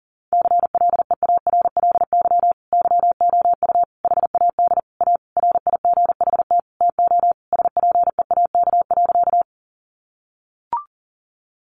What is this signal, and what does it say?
Morse code, heard as a keyed tone, about 40 wpm, 700 Hz. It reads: CLEARLY YOU HAD A RIGHT TO SPEAK.
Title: Sentences from Top 1300 Words 40wpm